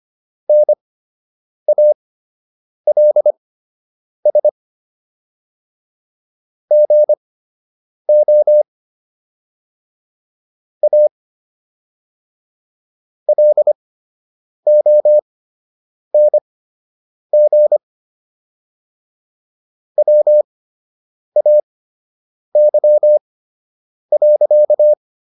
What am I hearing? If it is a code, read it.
Morse code, heard as a keyed tone, about 25 wpm, spaced as for 8 wpm, 600 Hz. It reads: NALS GO A LONG WAY.